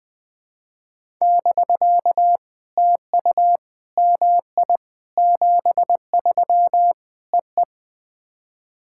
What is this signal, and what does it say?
Morse code, heard as a keyed tone, about 21 wpm, 700 Hz. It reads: <BK> TU MI 73 EE
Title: POTA QSO Practice - 20wpm